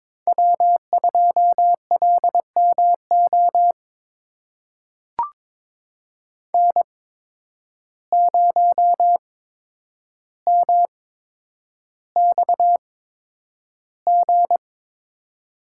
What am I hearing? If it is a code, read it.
Morse code, heard as a keyed tone, about 22 wpm, spaced as for 9 wpm, 700 Hz. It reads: W2LMO N0MXG
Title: USA Call Signs - 5 Characters - 8x Character Spacing - 22wpm